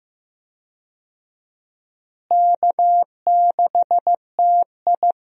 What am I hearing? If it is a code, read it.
Morse code, heard as a keyed tone, about 15 wpm, 700 Hz. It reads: K6TI
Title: USA Call Signs - 4 Characters - Repeated 2x - 15wpm